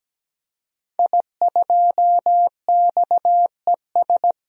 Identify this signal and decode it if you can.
Morse code, heard as a keyed tone, about 17 wpm, 700 Hz. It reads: I2XES
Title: DXCC-100 Call Signs - 5 Characters - 17wpm